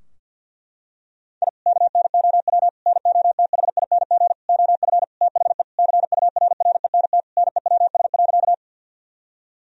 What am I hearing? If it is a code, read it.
Morse code, heard as a keyed tone, about 50 wpm, 700 Hz. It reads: I KNOW NOTHING OF THE CURRENT NEWS.
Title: Sentences from Top 1000 Words - Review Part 4 - 50wpm